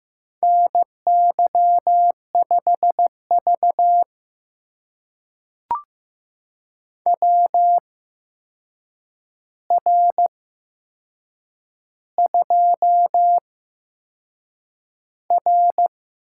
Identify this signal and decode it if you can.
Morse code, heard as a keyed tone, about 15 wpm, spaced as for 6 wpm, 700 Hz. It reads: NY5V WR2R
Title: USA Call Signs - 4 Characters - 8x Character Spacing - 15wpm